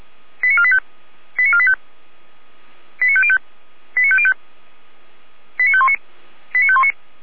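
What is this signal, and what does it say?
Five-tone selective call (ZVEI), heard as quick runs of five tones, tones of about 70 ms. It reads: 89485 89485 89505 89505 89519 89519
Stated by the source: ZVEI five-tone alert received by SDR.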